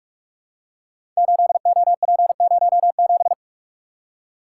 Morse code, heard as a keyed tone, about 44 wpm, 700 Hz. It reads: 8OP07
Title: Code Groups - 5 Letter:Numbers - 45wpm